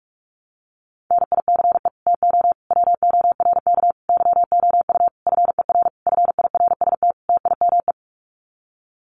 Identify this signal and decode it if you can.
Morse code, heard as a keyed tone, about 44 wpm, 700 Hz. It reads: NICE TO WORK YOU FER FIRST TIME
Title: Common QSO Phrases 45wpm